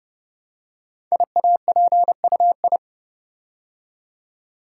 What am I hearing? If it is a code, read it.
Morse code, heard as a keyed tone, about 30 wpm, 700 Hz. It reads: IAPUI
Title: Code Groups - 5 Letters - 30wpm